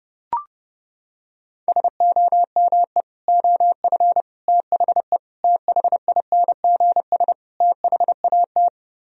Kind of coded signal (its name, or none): Morse code